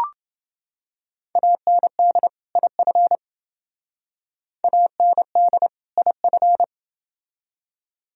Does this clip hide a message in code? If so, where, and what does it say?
Yes: Morse code from 1.4 to 6.6 s, heard as a keyed tone, about 29 wpm, 700 Hz. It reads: AND IF AND IF